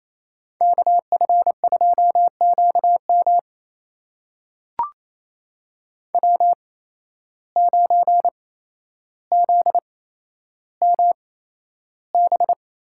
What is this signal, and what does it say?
Morse code, heard as a keyed tone, about 27 wpm, spaced as for 12 wpm, 700 Hz. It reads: KF2QM W9ZMB
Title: USA Call Signs - 5 Characters - 8x Character Spacing - 28wpm